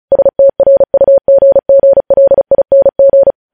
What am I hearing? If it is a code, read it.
Morse code, heard as a keyed tone, about 36 wpm, 550 Hz. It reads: STRUGGLING
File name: podcast / News headlines-1x-35x35